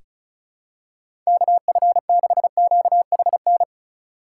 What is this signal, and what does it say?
Morse code, heard as a keyed tone, about 36 wpm, 700 Hz. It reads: KF6QHN
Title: USA Call Signs - 6 Characters - 3x Character Spacing - 35wpm